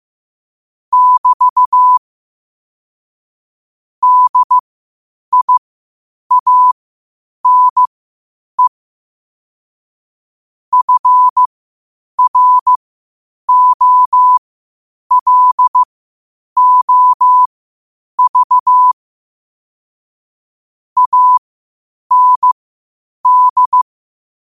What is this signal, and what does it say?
Morse code, heard as a keyed tone, about 15 wpm, spaced as for 8 wpm, 1000 Hz. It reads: <BT> DIANE FROLOV AND
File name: qotdfiles / 8wpm